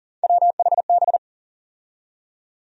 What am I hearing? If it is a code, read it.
Morse code, heard as a keyed tone, about 40 wpm, 700 Hz. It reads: WHB
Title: USA Call Sign Postfixes - 3 Letters 40wpm